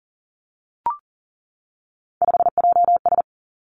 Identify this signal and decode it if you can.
Morse code, heard as a keyed tone, about 42 wpm, 700 Hz. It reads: 5JS
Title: Three Challenging Letter-Number Combinations 40wpm